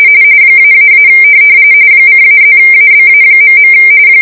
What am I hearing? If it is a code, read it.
RTTY, about 45.45 baud. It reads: --- :-+ABA..